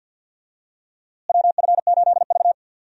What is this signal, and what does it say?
Morse code, heard as a keyed tone, about 52 wpm, 700 Hz. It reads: WF8V
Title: USA Call Signs - 4 Characters - Repeated 3x - 50wpm